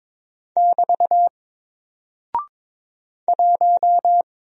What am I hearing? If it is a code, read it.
Morse code, heard as a keyed tone, about 22 wpm, 700 Hz. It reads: <BT> 1
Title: Single Character (Letter:Number:Punctuation:Prosign) Alt 22wpm